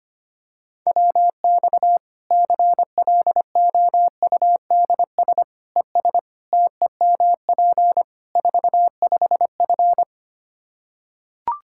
Morse code, heard as a keyed tone, about 25 wpm, 700 Hz. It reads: WX CLOUDS ES TEMP 45F
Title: Practice QSO - Exchange 2 - Info - 25wpm